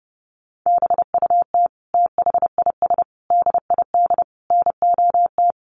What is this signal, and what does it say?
Morse code, heard as a keyed tone, about 30 wpm, 700 Hz. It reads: BUT THIS DID NOT
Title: Sentences from Top 1200 Words 30wpm